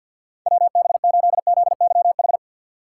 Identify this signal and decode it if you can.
Morse code, heard as a keyed tone, about 50 wpm, 700 Hz. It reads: WB8ZYH
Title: USA Call Signs - 6 Characters - 3x Character Spacing - 50wpm